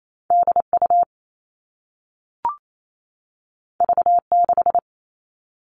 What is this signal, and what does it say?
Morse code, heard as a keyed tone, about 28 wpm, 700 Hz. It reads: DU V6
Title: All Cut Numbers - 2-Digit 28wpm